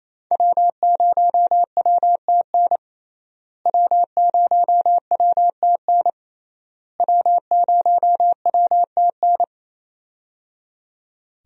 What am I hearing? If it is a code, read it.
Morse code, heard as a keyed tone, about 28 wpm, 700 Hz. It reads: W0WTN W0WTN W0WTN
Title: USA Call Signs - 5 Characters - Repeated 3x - 28wpm